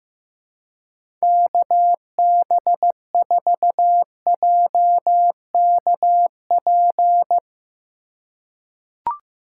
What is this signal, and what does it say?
Morse code, heard as a keyed tone, about 15 wpm, 700 Hz. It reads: KB4JKP
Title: USA Call Signs - 6 Characters - 4x Character Spacing - 15wpm